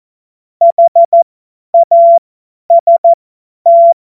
Morse code, heard as a keyed tone, about 14 wpm, spaced as for 10 wpm, 680 Hz. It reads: HAST
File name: Morsekurs / Witze / Witz 29. Juni 2021, 50 F70